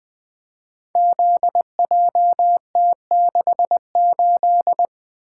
Morse code, heard as a keyed tone, about 20 wpm, 700 Hz. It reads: ZJT68
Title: Code Groups - 5 Letter:Numbers - 20wpm